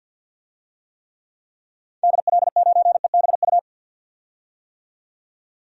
Morse code, heard as a keyed tone, about 52 wpm, 700 Hz. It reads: DL9EBU
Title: DXCC-100 Call Signs - 6 Characters - 50wpm